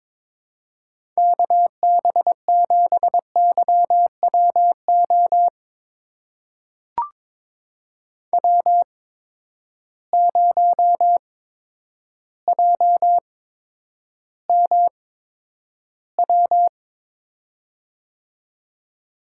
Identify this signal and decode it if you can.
Morse code, heard as a keyed tone, about 22 wpm, spaced as for 10 wpm, 700 Hz. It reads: KB7YWO W0JMW
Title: USA Call Signs - 8x Character Spacing - 22wpm